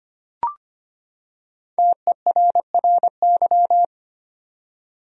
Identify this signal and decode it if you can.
Morse code, heard as a keyed tone, about 25 wpm, 700 Hz. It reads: TERRY
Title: USA Common Names - Max 5 Letters - 25wpm